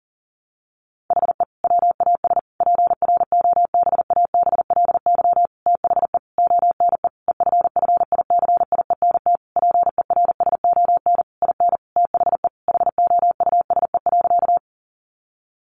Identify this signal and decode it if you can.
Morse code, heard as a keyed tone, about 40 wpm, 700 Hz. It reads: HE WAS PROBABLY THE ONE EFFICIENT PERSON IN THE HOUSE.
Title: Sentences from Top 1300 Words 40wpm